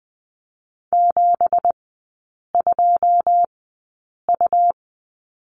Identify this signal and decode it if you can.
Morse code, heard as a keyed tone, about 20 wpm, spaced as for 8 wpm, 690 Hz. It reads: 72U
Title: 212 Lesson 34e - Number 8 - Three Characters 20wpm